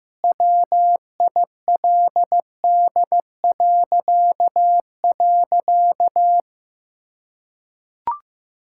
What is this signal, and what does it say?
Morse code, heard as a keyed tone, about 15 wpm, 700 Hz. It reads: WILD..
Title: Pangram Challenge v2 - 15wpm